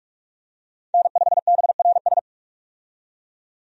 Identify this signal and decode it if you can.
Morse code, heard as a keyed tone, about 46 wpm, 700 Hz. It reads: N5BRS
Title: USA Call Signs - 5 Characters - 45wpm